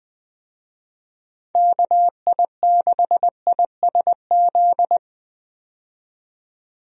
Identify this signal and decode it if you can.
Morse code, heard as a keyed tone, about 21 wpm, 700 Hz. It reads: KI6ISZ
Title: USA Call Signs - 6 Characters - 20wpm